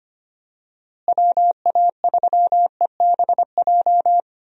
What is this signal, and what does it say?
Morse code, heard as a keyed tone, about 25 wpm, 700 Hz. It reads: WA3EBJ
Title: USA Call Signs - 6 Characters - Repeated 3x - 25wpm